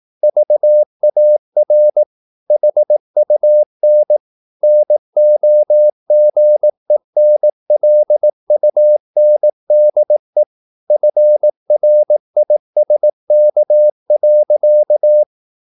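Morse code, heard as a keyed tone, about 18 wpm, 600 Hz. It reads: VAR HUN NOGENLUNDE FRISK.
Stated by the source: Slow (QRS) Morse practice at 18 wpm, CW med Gnister.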